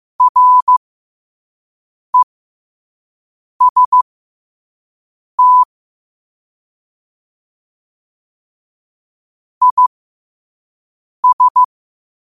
Morse code, heard as a keyed tone, about 15 wpm, spaced as for 5 wpm, 1000 Hz. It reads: REST IS
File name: qotdfiles / 5wpm